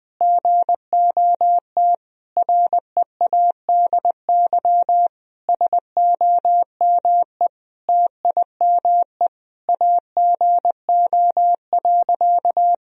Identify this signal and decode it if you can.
Morse code, heard as a keyed tone, about 20 wpm, 700 Hz. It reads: GOT READY SOME TIME AGO.
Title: Sentences from Top 1000 Words - Review Part 3 - 20wpm